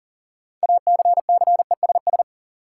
Morse code, heard as a keyed tone, about 40 wpm, 700 Hz. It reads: ACCESS
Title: Sentences from Top 1000 Words - Review Part 4 - New Word Drill - 40wpm